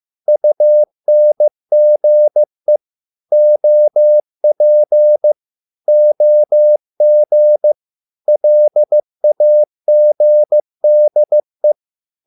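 Morse code, heard as a keyed tone, about 15 wpm, 600 Hz. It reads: UNGE OP OG LAGDE
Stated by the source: Morse code at a slow 15 wpm from CW med Gnister.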